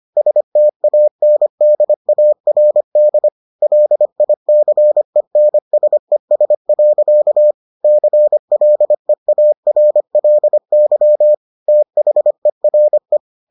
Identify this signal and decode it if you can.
Morse code, heard as a keyed tone, about 25 wpm, 600 Hz. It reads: STANDARD LICENSES. CLEARLY THERE